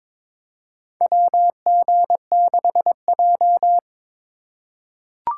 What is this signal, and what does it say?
Morse code, heard as a keyed tone, about 22 wpm, 700 Hz. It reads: WG6J
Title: USA Call Signs - 4x Character Spacing - 22wpm